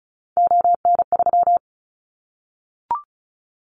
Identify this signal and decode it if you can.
Morse code, heard as a keyed tone, about 36 wpm, 690 Hz. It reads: ON3
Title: Callsign Prefixes for Top 50 DXCC - 35wpm